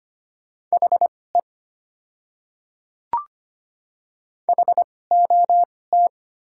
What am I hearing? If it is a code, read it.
Morse code, heard as a keyed tone, about 25 wpm, spaced as for 18 wpm, 700 Hz. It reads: HE HOT
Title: 2x Character Spacing - Top 200 Words - 25wpm